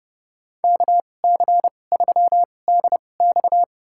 Morse code, heard as a keyed tone, about 30 wpm, spaced as for 22 wpm, 700 Hz. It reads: KC3DX
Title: USA Call Signs - 5 Characters - 2x Character Spacing - 30wpm